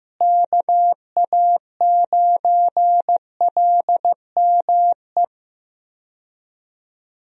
Morse code, heard as a keyed tone, about 15 wpm, 700 Hz. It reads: KA9LME
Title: USA Call Signs - 6 Characters - Repeated 3x - 15wpm